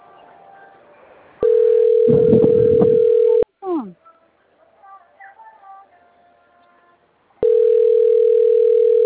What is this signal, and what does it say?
Ringback tone (North America), 440 + 480 Hz, with speech.
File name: out- - - - - .45969